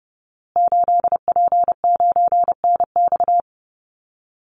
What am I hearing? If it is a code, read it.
Morse code, heard as a keyed tone, about 30 wpm, 700 Hz. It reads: 8P9NX
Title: Call Signs Repeated 2x - 30wpm